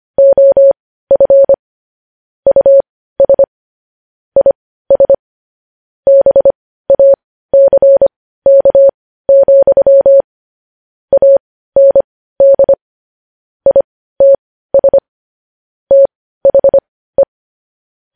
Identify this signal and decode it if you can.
Morse code, heard as a keyed tone, about 25 wpm, spaced as for 15 wpm, 550 Hz. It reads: OF US IS BACK, AND ITS THE
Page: news-headlines-in-morse-code-at-15-wpm